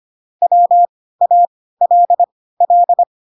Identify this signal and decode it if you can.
Morse code, heard as a keyed tone, about 25 wpm, spaced as for 16 wpm, 700 Hz. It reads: WALL
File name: Pinocchio / Pinocchio ch3 10 25 ik1hge